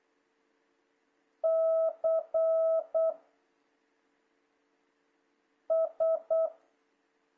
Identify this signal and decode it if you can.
Morse code, heard as a keyed tone, about 8 wpm, 660 Hz. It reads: C S
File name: jh8cbh / 2026cw